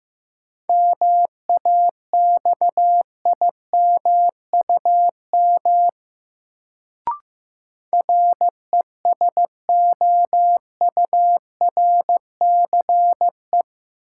Morse code, heard as a keyed tone, about 15 wpm, 700 Hz. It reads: MAXIMUM RESOURCE
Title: Sentences from Top 1300 Words - New Word Drill Alt - 15wpm